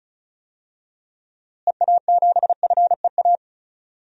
Morse code, heard as a keyed tone, about 36 wpm, 700 Hz. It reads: EA7FEA
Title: DXCC-100 Call Signs - 6 Characters - Repeated 3x - 35wpm